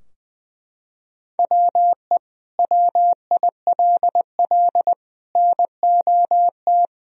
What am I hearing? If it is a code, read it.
Morse code, heard as a keyed tone, about 20 wpm, 700 Hz. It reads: WE WILL NOT
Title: Sentences from Top 1000 Words - Review Part 2 - 20wpm